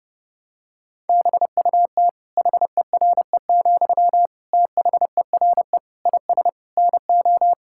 Morse code, heard as a keyed tone, about 30 wpm, 700 Hz. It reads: BUT HERE, THERE IS NO
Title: Sentences from Top 1000 Words - Review Part 6 - 30wpm